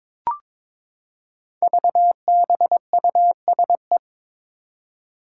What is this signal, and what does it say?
Morse code, heard as a keyed tone, about 22 wpm, 700 Hz. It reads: VBUSE
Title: Five Challenging Letter-Number Combinations 22wpm